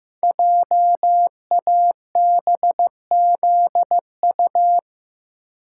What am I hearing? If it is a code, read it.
Morse code, heard as a keyed tone, about 15 wpm, 700 Hz. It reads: JABZU
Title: Code Groups - 5 Letters - 15wpm